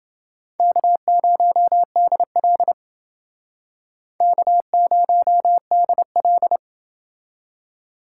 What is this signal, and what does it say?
Morse code, heard as a keyed tone, about 28 wpm, 700 Hz. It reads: K0DL K0DL